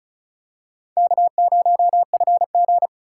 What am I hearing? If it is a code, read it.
Morse code, heard as a keyed tone, about 34 wpm, 700 Hz. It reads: K0FG
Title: USA Call Signs - 4 Characters - 4x Character Spacing - 35wpm